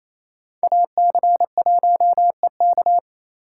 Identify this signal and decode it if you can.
Morse code, heard as a keyed tone, about 36 wpm, 700 Hz. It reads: AC1EK